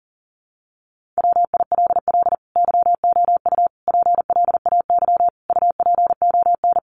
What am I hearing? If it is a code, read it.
Morse code, heard as a keyed tone, about 40 wpm, 700 Hz. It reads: WILL YOU PLAY UPON
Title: Sentences from Top 1500 Words 40wpm